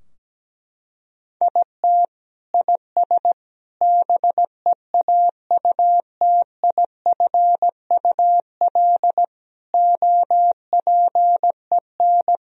Morse code, heard as a keyed tone, about 17 wpm, 700 Hz. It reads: IT IS BEAUTIFUL OPEN
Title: Sentences from Top 1000 Words - Review Part 2 - 17wpm